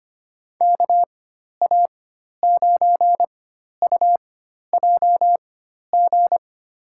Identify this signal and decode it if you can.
Morse code, heard as a keyed tone, about 25 wpm, spaced as for 12 wpm, 700 Hz. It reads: KA9UJG
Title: USA Call Signs - 6 Characters - 4x Character Spacing - 25wpm